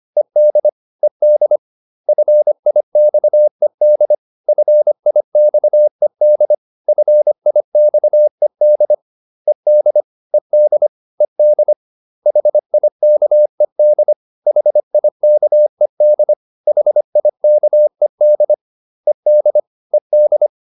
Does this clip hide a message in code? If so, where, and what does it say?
Yes: Morse code from 0.2 to 20.5 s, heard as a keyed tone, about 25 wpm, 600 Hz. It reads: ED ED FIXED FIXED FIXED ED ED ED HIKED HIKED HIKED ED ED